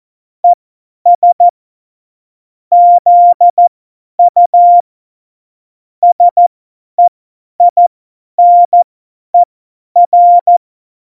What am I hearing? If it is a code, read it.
Morse code, heard as a keyed tone, about 14 wpm, spaced as for 10 wpm, 700 Hz. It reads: ES ZU SEINER